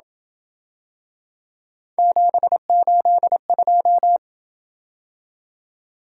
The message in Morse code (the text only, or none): 782